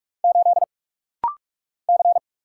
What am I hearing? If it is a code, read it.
Morse code, heard as a keyed tone, about 46 wpm, 700 Hz. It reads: 8 C
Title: Single Letter-Number - Rapid-Fire 45wpm